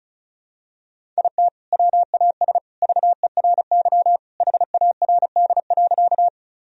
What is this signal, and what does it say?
Morse code, heard as a keyed tone, about 34 wpm, 700 Hz. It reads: IT WAS VERY HARD.
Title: Sentences from Top 1000 Words - Review Part 1 - 35wpm